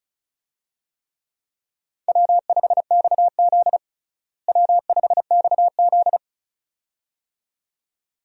Morse code, heard as a keyed tone, about 36 wpm, 700 Hz. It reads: W5XZ W5XZ
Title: USA Call Signs - 4 Characters - Repeated 2x - 35wpm